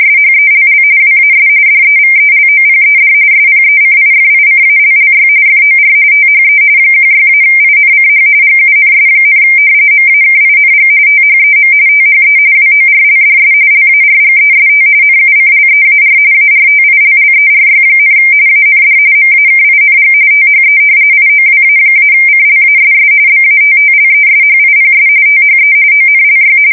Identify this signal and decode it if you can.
RTTY, about 45.45 baud. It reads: SHELL WAS DEVELOPED SPECIFICALLY FOR THE 16-INCH GUNS, MAKING THEM THE WORLDS LARGEST NUCLEAR ARTILLERY, AND MAKING THE IOWA-CLASS BATTLESHIPS THE ONLY
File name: fsk170 / battleshipnj